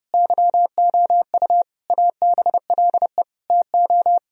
Morse code, heard as a keyed tone, about 32 wpm, 700 Hz. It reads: YOU ABLE TO
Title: Sentences from Top 1000 Words - Review Part 3 - 30wpm